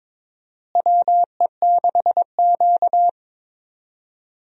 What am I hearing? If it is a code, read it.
Morse code, heard as a keyed tone, about 22 wpm, 700 Hz. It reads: WE6Q
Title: USA Call Signs - 4 Characters - Repeated 2x - 22wpm